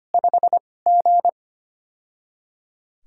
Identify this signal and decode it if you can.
Morse code, heard as a keyed tone, about 25 wpm, 700 Hz. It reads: 5 G
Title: USA Call Signs - 4 Characters - 2x Character Spacing - 25wpm